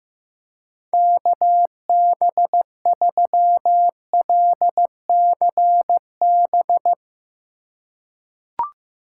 Morse code, heard as a keyed tone, about 15 wpm, 700 Hz. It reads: KB3LCB
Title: USA Call Signs - 6 Characters - 4x Character Spacing - 15wpm